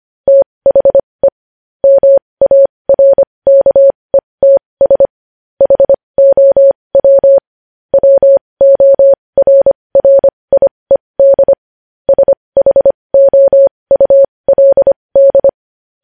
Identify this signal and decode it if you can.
Morse code, heard as a keyed tone, about 25 wpm, spaced as for 20 wpm, 550 Hz. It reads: THE MARKETS HOW WORRIED SHOULD